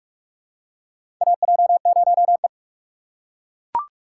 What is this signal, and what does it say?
Morse code, heard as a keyed tone, about 43 wpm, 700 Hz. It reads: AJ0E
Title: USA Call Signs - 4 Characters - 4x Character Spacing - 45wpm